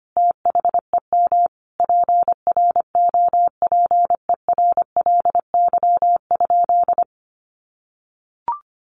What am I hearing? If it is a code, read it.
Morse code, heard as a keyed tone, about 25 wpm, 700 Hz. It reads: THEM PROPERLY?
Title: Sentences from Top 1100 Words 25wpm